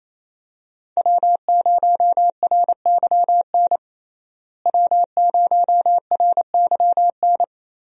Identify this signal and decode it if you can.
Morse code, heard as a keyed tone, about 27 wpm, 700 Hz. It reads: W0RYN W0RYN
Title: USA Call Signs - 5 Characters - Repeated 3x - 28wpm